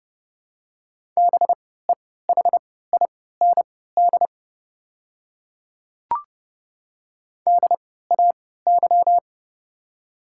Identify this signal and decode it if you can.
Morse code, heard as a keyed tone, about 30 wpm, spaced as for 17 wpm, 700 Hz. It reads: BEHIND DAY
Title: 3x Character Spacing - Top 500 Words - 30wpm